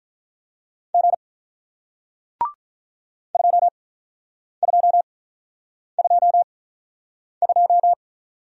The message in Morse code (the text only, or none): G 2222